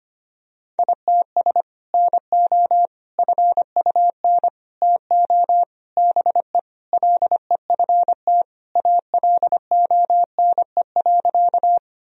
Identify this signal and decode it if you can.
Morse code, heard as a keyed tone, about 25 wpm, 700 Hz. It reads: ITS NO FUN TO BE LEFT ALONE.
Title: Sentences from Top 1000 Words - Review Part 4 - 25wpm